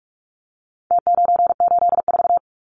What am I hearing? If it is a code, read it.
Morse code, heard as a keyed tone, about 44 wpm, 700 Hz. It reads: T984
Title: Numbers - Contest Style - 4-Digit 45wpm